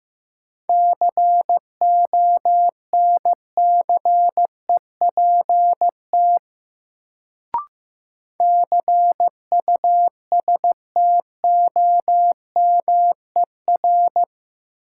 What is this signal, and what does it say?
Morse code, heard as a keyed tone, about 15 wpm, 700 Hz. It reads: CONCEPT CUSTOMER
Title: Sentences from Top 1100 Words - New Word Drill - 15wpm